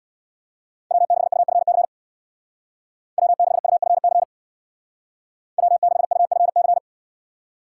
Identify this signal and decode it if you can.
Morse code, heard as a keyed tone, about 64 wpm, 700 Hz. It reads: W6UFC W6UFC W6UFC